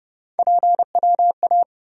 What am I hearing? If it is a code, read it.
Morse code, heard as a keyed tone, about 31 wpm, 700 Hz. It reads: PWA
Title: USA Call Sign Postfixes - 3 Letters 30wpm